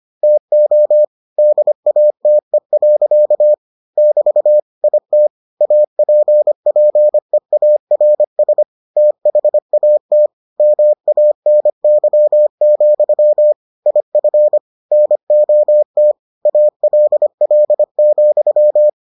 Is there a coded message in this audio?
Yes: Morse code from 0.2 to 18.9 s, heard as a keyed tone, about 25 wpm, 600 Hz. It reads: TO DATE. <BT> IT APPEARS THAT MANY, IF NOT ALL,